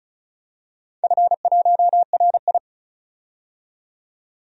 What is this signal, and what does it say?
Morse code, heard as a keyed tone, about 36 wpm, 700 Hz. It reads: F1RI
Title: DXCC-100 Call Signs - 4 Characters - 35wpm